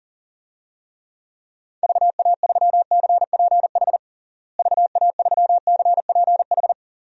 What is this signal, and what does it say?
Morse code, heard as a keyed tone, about 37 wpm, 700 Hz. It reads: VA3CPH VA3CPH
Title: DXCC-100 Call Signs - 6 Characters - Repeated 3x - 40wpm